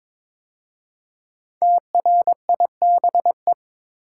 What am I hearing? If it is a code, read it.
Morse code, heard as a keyed tone, about 22 wpm, 700 Hz. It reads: TRIBE
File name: 5 Letter Words - First 3 Letters Given - 22wpm